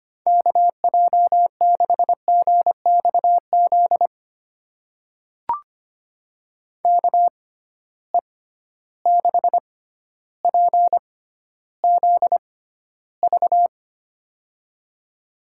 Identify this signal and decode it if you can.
Morse code, heard as a keyed tone, about 25 wpm, spaced as for 13 wpm, 700 Hz. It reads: KJ6GXZ KE6PZV
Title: USA Call Signs - 6 Characters - 6x Character Spacing - 25wpm